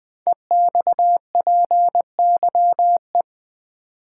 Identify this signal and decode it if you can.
Morse code, heard as a keyed tone, about 20 wpm, 700 Hz. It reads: EXPYE
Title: Code Groups - 5 Letters - 20wpm